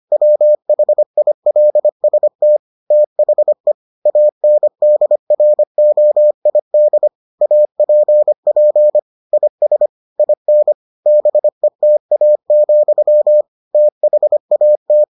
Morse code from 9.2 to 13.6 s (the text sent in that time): IS IN BETA,